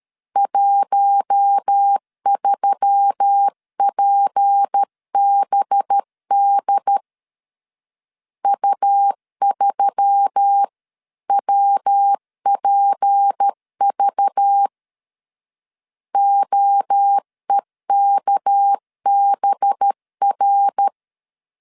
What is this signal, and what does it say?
Morse code, heard as a keyed tone, about 13 wpm, 800 Hz. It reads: 13PBD U3WPV OEKBR
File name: CW / DET / 1 6 1500C S C 12wpm 3533